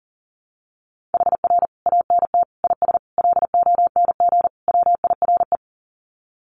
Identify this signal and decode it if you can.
Morse code, heard as a keyed tone, about 40 wpm, 700 Hz. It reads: HR ANT IS LONG WIRE
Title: Common QSO Phrases 40wpm